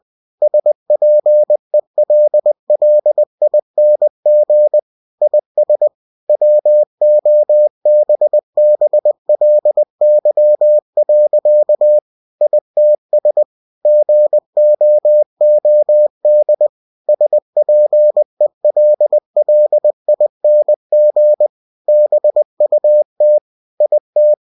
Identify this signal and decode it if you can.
Morse code, heard as a keyed tone, about 20 wpm, 600 Hz. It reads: SPELLING IS WOBBLY. ITS GOOD SPELLING BUT IT
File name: cw-books / winnie-the-pooh / winnie-20wpm-page-0051